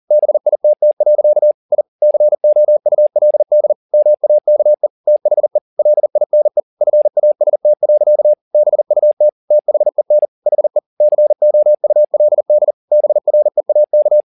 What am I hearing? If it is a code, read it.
Morse code, heard as a keyed tone, about 40 wpm, 600 Hz. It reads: BITT. I COULD MAKE THE LINE FAST. BUT THEN HE COULD BREAK